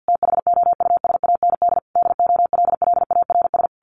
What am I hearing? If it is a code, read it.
Morse code, heard as a keyed tone, about 50 wpm, 700 Hz. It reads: THOUSAND DOLLARS
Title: A Sound Of Thunder 50wpm